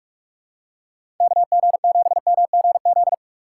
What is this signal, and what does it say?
Morse code, heard as a keyed tone, about 46 wpm, 700 Hz. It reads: KG7MGZ
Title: USA Call Signs - 6x Character Spacing - 45wpm